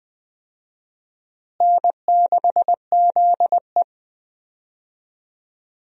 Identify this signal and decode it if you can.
Morse code, heard as a keyed tone, about 20 wpm, 700 Hz. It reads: N6ZE
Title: USA Call Signs - 4 Characters - Repeated 3x - 20wpm